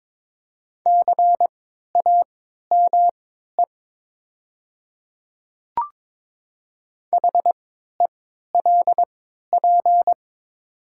Morse code, heard as a keyed tone, about 22 wpm, spaced as for 13 wpm, 700 Hz. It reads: CAME HELP
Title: 3x Character Spacing - Top 300 Words - 22wpm